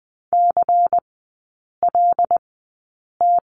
Morse code, heard as a keyed tone, about 20 wpm, spaced as for 8 wpm, 690 Hz. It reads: CLT
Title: 093 Lesson 16e - Letter c - Three Characters 20wpm